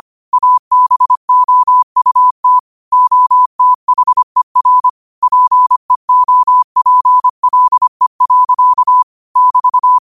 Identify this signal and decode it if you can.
Morse code, heard as a keyed tone, about 25 wpm, 1000 Hz. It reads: ABOUT OTHER PEOPLE. <BT>